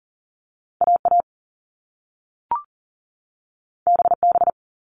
Morse code, heard as a keyed tone, about 40 wpm, 700 Hz. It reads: AA BB
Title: All Cut Numbers - 2-Digit 40wpm